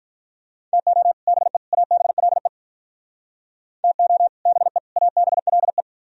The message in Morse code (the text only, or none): TO BE ABLE TO BE ABLE